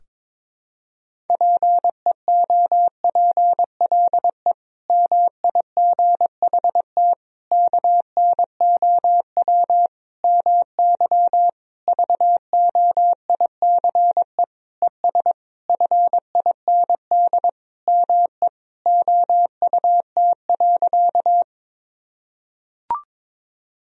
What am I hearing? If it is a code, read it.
Morse code, heard as a keyed tone, about 22 wpm, 700 Hz. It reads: PEOPLE MIGHT KNOW MY VOICE ES FIND ME OUT.
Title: Sentences from Top 500 Words and Abbreviations 22wpm